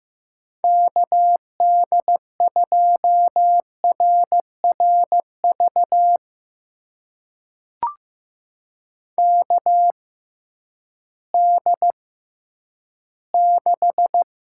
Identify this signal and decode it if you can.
Morse code, heard as a keyed tone, about 15 wpm, spaced as for 10 wpm, 700 Hz. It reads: KD2RRV KD6
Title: USA Call Signs - 6 Characters - 6x Character Spacing - 15wpm